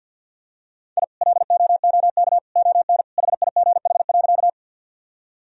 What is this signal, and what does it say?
Morse code, heard as a keyed tone, about 47 wpm, 700 Hz. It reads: I LOOK ON HIGH.
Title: Sentences from Top 1000 Words - Review Part 1 - 50wpm